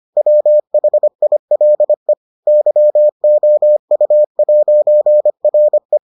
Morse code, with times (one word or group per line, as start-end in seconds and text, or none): 0.2-2.1 WHILE
2.5-6.0 YOU'RE